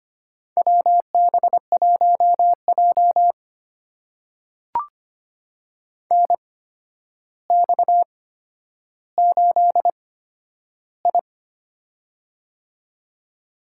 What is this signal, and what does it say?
Morse code, heard as a keyed tone, about 25 wpm, spaced as for 11 wpm, 700 Hz. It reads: WB1J NX8I
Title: USA Call Signs - 4 Characters - 8x Character Spacing - 25wpm